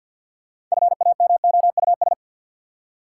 Morse code, heard as a keyed tone, about 52 wpm, 700 Hz. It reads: FAMOUS
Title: Six Letter Words - 50wpm